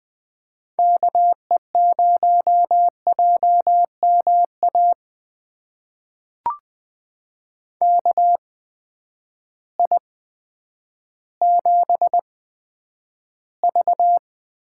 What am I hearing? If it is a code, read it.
Morse code, heard as a keyed tone, about 20 wpm, spaced as for 10 wpm, 700 Hz. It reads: KE0JMA KI7V
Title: USA Call Signs - 6 Characters - 8x Character Spacing - 20wpm